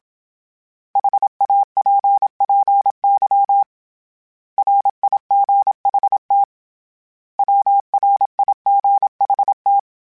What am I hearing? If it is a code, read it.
Morse code, heard as a keyed tone, about 27 wpm, 800 Hz. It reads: HAPPY RIGHT WRIGHT
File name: Misc / Misc-mp3s / Words @25